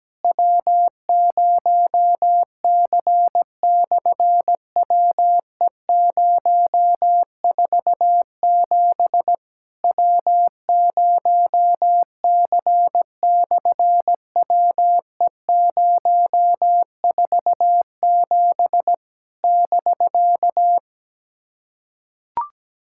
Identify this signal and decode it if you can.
Morse code, heard as a keyed tone, about 17 wpm, 700 Hz. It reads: W0C/WE047 W0C/WE047 <BK>
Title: SOTA QSO Practice 17wpm